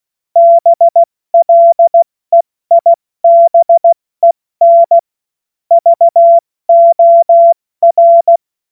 Morse code, heard as a keyed tone, about 16 wpm, spaced as for 14 wpm, 680 Hz. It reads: BLEIBEN VOR
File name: Witz-30-September-2025-70-F80